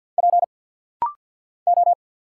Morse code, heard as a keyed tone, about 49 wpm, 700 Hz. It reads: P O
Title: Single Letters - Rapid-Fire 50wpm